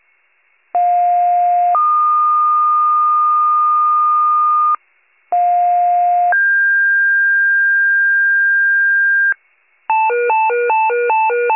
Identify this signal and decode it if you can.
Two-tone page, heard as a one-second tone, then a three-second tone, about 710 then 1220 Hz, about 710 then 1600 Hz, with noise. It is heard from 0.7 to 9.3 s.